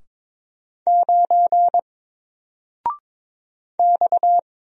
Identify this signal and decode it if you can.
Morse code, heard as a keyed tone, about 22 wpm, 700 Hz. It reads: 9 X
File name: Single-Letter-Number-22wpm